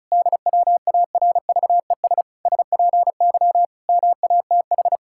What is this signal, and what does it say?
Morse code, heard as a keyed tone, about 36 wpm, 700 Hz. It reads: DWARVES SPY MATH
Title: Pangram Challenge v2 - 35wpm